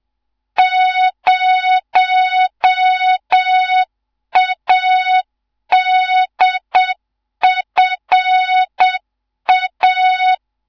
Morse code, heard as a keyed tone, about 7 wpm, 3700 Hz. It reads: 0ADFA
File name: Morze-HA2MN / hang / 15 GYAKORLAT